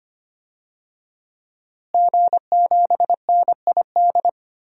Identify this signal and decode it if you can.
Morse code, heard as a keyed tone, about 25 wpm, 700 Hz. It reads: G7NID
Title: DXCC-100 Call Signs - 5 Characters - Repeated 3x - 25wpm